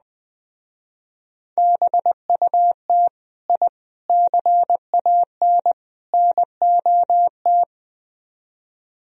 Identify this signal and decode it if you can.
Morse code, heard as a keyed tone, about 20 wpm, 700 Hz. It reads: BUT I CAN NOT